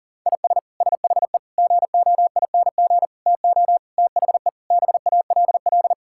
Morse code, heard as a keyed tone, about 40 wpm, 700 Hz. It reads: IS SHE GOING TO THE BALL
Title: Sentences from Top 1000 Words - Review Part 6 - 40wpm